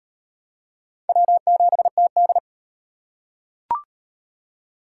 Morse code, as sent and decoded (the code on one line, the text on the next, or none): .-- --... - -..
W7TD